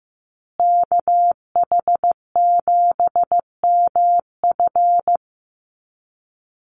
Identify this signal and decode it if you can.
Morse code, heard as a keyed tone, about 15 wpm, 700 Hz. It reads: KH7MF
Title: USA Call Signs Repeated 2x 15wpm